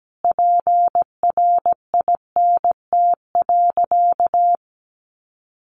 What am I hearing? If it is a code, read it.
Morse code, heard as a keyed tone, about 17 wpm, 700 Hz. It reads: PRINT.
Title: Sentences from Top 1400 Words 17wpm